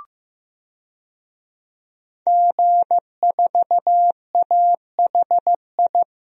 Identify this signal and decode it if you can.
Morse code, heard as a keyed tone, about 15 wpm, 700 Hz. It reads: G4AHI
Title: DXCC-100 Call Signs - 5 Characters - 15wpm